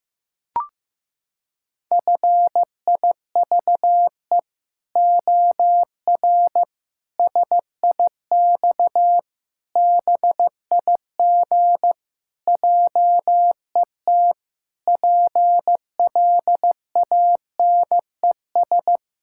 Morse code, heard as a keyed tone, about 15 wpm, 700 Hz. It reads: FIVE OR SIX BIG JET PLANES
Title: Pangram Challenge v2 - 15wpm